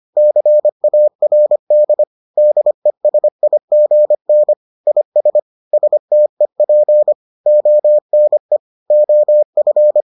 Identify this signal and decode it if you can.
Morse code, heard as a keyed tone, about 25 wpm, 600 Hz. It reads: CARD DESIGN IS STEP ONE OF